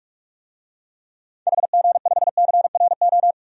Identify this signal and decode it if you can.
Morse code, heard as a keyed tone, about 46 wpm, 700 Hz. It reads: HG5CRO
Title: DXCC-100 Call Signs - 6 Characters - Repeated 3x - 45wpm